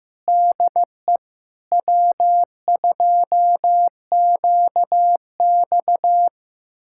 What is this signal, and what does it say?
Morse code, heard as a keyed tone, about 15 wpm, 700 Hz. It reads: DE W2QX
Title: USA Callsign DE Callsign - 4 Characters - 15wpm